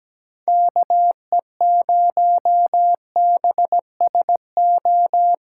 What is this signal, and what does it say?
Morse code, heard as a keyed tone, about 17 wpm, 700 Hz. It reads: KE0BSO
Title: USA Callsign DE Callsign - 6 Characters - 17wpm